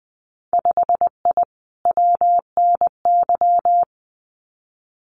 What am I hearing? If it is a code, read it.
Morse code, heard as a keyed tone, about 20 wpm, 700 Hz. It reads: 5I WNY
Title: Winter Field Day Exchanges Encore 20wpm